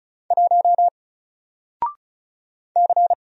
Morse code, heard as a keyed tone, about 36 wpm, 700 Hz. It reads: 1 C
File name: Single-Letter-Number-35wpm